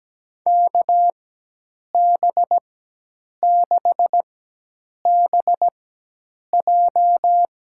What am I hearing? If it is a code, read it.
Morse code, heard as a keyed tone, about 17 wpm, spaced as for 8 wpm, 700 Hz. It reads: KB6BJ
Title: USA Call Signs - 4x Character Spacing - 17wpm